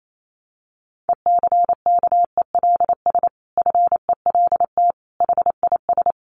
Morse code, heard as a keyed tone, about 28 wpm, 700 Hz. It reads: ECKELS FELT HIS
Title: A Sound Of Thunder 28wpm